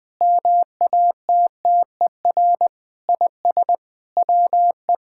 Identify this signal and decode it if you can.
Morse code, heard as a keyed tone, about 20 wpm, 700 Hz. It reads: MATTER IS WE
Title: Sentences with 5-Word Phrases - Spoken Beginning - 20wpm